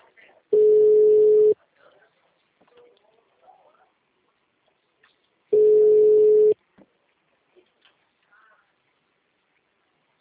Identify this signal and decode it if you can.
Ringback tone (Europe, 425 Hz), with speech.
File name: out- -2004- - - .136